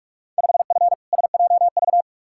Morse code, heard as a keyed tone, about 46 wpm, 700 Hz. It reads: 5F SJV
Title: Field Day Exchanges Encore - 45wpm